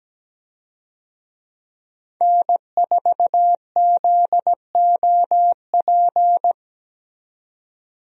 Morse code, heard as a keyed tone, about 17 wpm, 700 Hz. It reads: N4ZOP
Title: USA Call Signs - 5 Characters - 17wpm